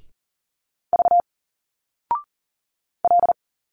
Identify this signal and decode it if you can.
Morse code, heard as a keyed tone, about 40 wpm, 700 Hz. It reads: V L
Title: Single Letters 40wpm